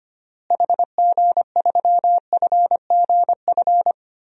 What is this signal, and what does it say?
Morse code, heard as a keyed tone, about 25 wpm, 700 Hz. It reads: HG3FGF
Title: DXCC-100 Call Signs - 6 Characters - Repeated 3x - 25wpm